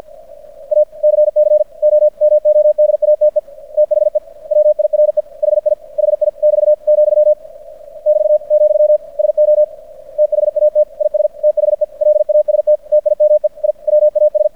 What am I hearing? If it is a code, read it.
Morse code, heard as a keyed tone, about 50 wpm, 600 Hz. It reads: ACCOMODATE THE WIRE HI HI <BT> , <BT> , SO THAT IS THE LAST TIME I WAS
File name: archived / CRSnet